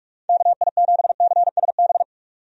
Morse code, heard as a keyed tone, about 45 wpm, 700 Hz. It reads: KI7CSB
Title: USA Call Signs - 6 Characters - 45wpm